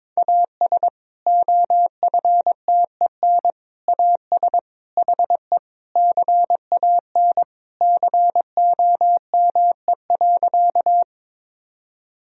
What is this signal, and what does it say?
Morse code, heard as a keyed tone, about 22 wpm, 700 Hz. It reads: AS OFTEN AS HE CAN COME.
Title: Sentences from Top 1000 Words - Review Part 1 - 22wpm